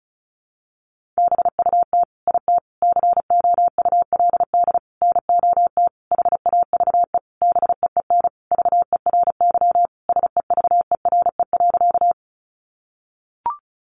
Morse code, heard as a keyed tone, about 34 wpm, 700 Hz. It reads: BUT IT COULD NOT HAVE BEEN VERY SEVERE.
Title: Sentences from Top 1400 Words 35wpm